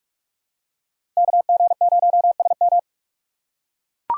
Morse code, heard as a keyed tone, about 42 wpm, 700 Hz. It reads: KG0SM